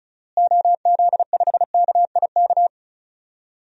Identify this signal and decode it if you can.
Morse code, heard as a keyed tone, about 36 wpm, 700 Hz. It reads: OZ5KIK
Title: DXCC-100 Call Signs - 6 Characters - Repeated 2x - 35wpm